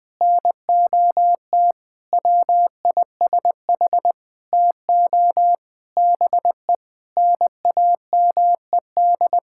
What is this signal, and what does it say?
Morse code, heard as a keyed tone, about 20 wpm, 700 Hz. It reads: NOT WISH TO BE NAMED
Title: Sentences with 7-Word Phrases - Spoken Beginning - 20wpm